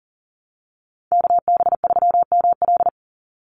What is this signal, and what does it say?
Morse code, heard as a keyed tone, about 40 wpm, 700 Hz. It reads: KB3ML
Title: Call Signs - 40wpm